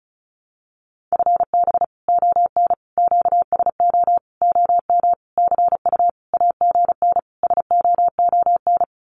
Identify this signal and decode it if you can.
Morse code, heard as a keyed tone, about 34 wpm, 700 Hz. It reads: FB ON QSO OM CU AGN SOON
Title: Common QSO Phrases 35wpm